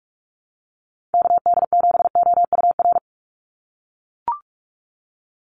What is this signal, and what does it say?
Morse code, heard as a keyed tone, about 44 wpm, 700 Hz. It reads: KD7OUR
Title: USA Call Signs 45wpm